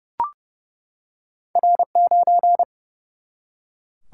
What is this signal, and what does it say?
Morse code, heard as a keyed tone, about 29 wpm, 700 Hz. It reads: R9
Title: Single-Letter Single-Number 30wpm